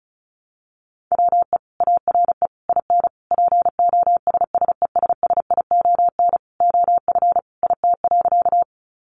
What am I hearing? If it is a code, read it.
Morse code, heard as a keyed tone, about 34 wpm, 700 Hz. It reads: WE ARE IN POSSESSION OF IT.
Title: Sentences from Top 1400 Words 35wpm